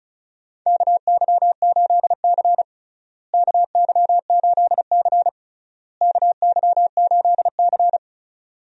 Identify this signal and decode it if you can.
Morse code, heard as a keyed tone, about 36 wpm, 700 Hz. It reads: KY8C KY8C KY8C
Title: USA Call Signs - 4 Characters - Repeated 3x - 35wpm